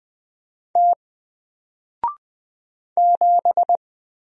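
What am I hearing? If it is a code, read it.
Morse code, heard as a keyed tone, about 20 wpm, 700 Hz. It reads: T 7
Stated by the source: Single letter and number Morse drill at 20 wpm, from N4LCD.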